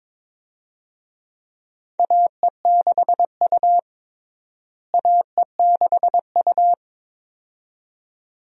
Morse code, heard as a keyed tone, about 22 wpm, 700 Hz. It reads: AE6U AE6U
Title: USA Call Signs - 4 Characters - Repeated 2x - 22wpm